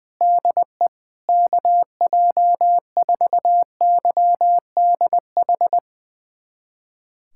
Morse code, heard as a keyed tone, about 20 wpm, 700 Hz. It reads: DE KJ4YDH
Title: USA Callsign DE Callsign 20wpm